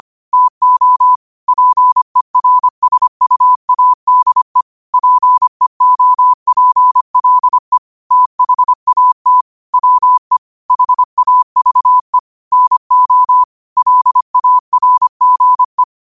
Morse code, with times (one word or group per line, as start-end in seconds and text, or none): 0.3-1.2 TO
1.5-4.6 PERSUADE
4.9-7.8 PEOPLE
8.1-9.4 THAT
9.7-10.4 WE
10.7-12.2 HAVE
12.5-13.4 NO
13.8-15.8 LARGE